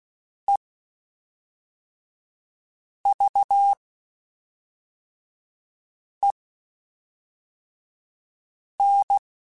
Morse code, heard as a keyed tone, about 16 wpm, spaced as for 3 wpm, 800 Hz. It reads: EVEN
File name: Spaced / Chuckles Spaced-mp3s / HumanErr sp @5